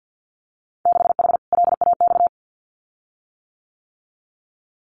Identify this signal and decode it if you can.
Morse code, heard as a keyed tone, about 50 wpm, 700 Hz. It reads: 6H LAX
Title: Winter Field Day Exchanges Encore 50wpm